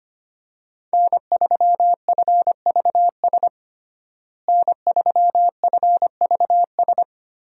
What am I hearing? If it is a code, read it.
Morse code, heard as a keyed tone, about 25 wpm, 700 Hz. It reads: N3FVS N3FVS
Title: USA Call Signs - 5 Characters - Repeated 2x - 25wpm